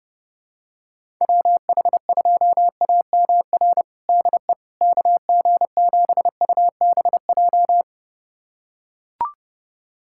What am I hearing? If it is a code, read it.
Morse code, heard as a keyed tone, about 30 wpm, 700 Hz. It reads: WH2AMR DE KG7UBJ
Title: USA Callsign DE Callsign - 6 Characters - 30wpm